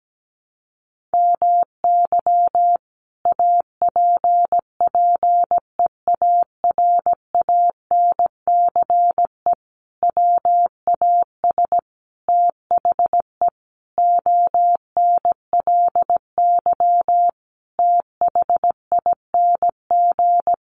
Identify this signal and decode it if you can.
Morse code, heard as a keyed tone, about 17 wpm, 700 Hz. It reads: MY APPEARANCE WAS THE ONLY THING
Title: Sentences from Top 1100 Words 17wpm